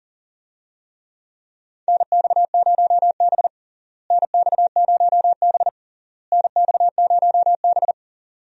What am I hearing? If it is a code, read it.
Morse code, heard as a keyed tone, about 40 wpm, 700 Hz. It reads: NX0B NX0B NX0B
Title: USA Call Signs - 4 Characters - Repeated 3x - 40wpm